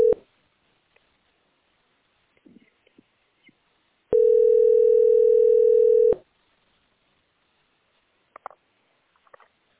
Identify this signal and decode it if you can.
Telephone call progress tone (North America), heard as a telephone network's tone, about 440, 480 Hz. Ringback tone.